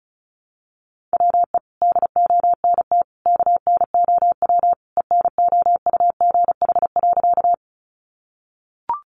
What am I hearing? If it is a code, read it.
Morse code, heard as a keyed tone, about 34 wpm, 700 Hz. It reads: WE DONT KNOW ENOUGH.
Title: Sentences from Top 400 Words 35wpm